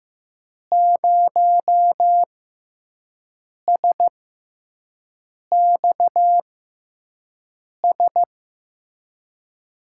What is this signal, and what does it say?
Morse code, heard as a keyed tone, about 15 wpm, spaced as for 5 wpm, 700 Hz. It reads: 0SXS
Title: USA Call Signs - 6 Characters - 6x Character Spacing - 15wpm